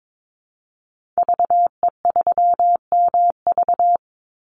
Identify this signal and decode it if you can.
Morse code, heard as a keyed tone, about 22 wpm, 700 Hz. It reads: VE3MV
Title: Call Signs Repeated 3x - 22wpm